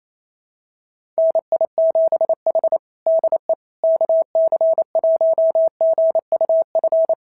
Morse code, heard as a keyed tone, about 28 wpm, 650 Hz. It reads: NI7H DE KC1GUF
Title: SOTA QSO Practice 28wpm